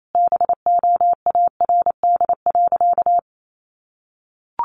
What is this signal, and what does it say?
Morse code, heard as a keyed tone, about 28 wpm, 700 Hz. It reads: BOARD.
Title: Sentences from Top 1400 Words 28wpm